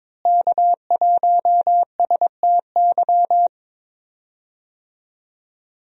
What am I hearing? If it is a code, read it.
Morse code, heard as a keyed tone, about 21 wpm, 700 Hz. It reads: K1STY
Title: USA Call Signs - 5 Characters - Repeated 3x - 22wpm